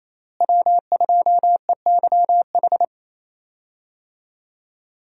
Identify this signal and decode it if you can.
Morse code, heard as a keyed tone, about 28 wpm, 700 Hz. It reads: W2EYH
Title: USA Call Signs - 5 Characters - Repeated 2x - 28wpm